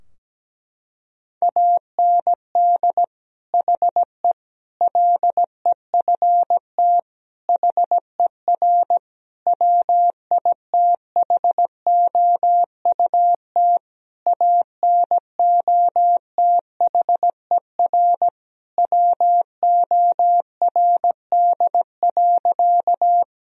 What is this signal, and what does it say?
Morse code, heard as a keyed tone, about 17 wpm, 700 Hz. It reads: AND HE LEFT HER WITHOUT ANOTHER WORD.
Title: Sentences from Top 1000 Words - Review Part 3 - 17wpm